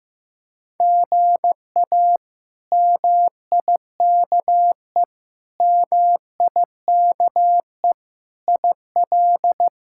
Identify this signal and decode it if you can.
Morse code, heard as a keyed tone, about 15 wpm, 700 Hz. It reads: GA MIKE MIKE IL
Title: K1USN SST QSO Practice - 15wpm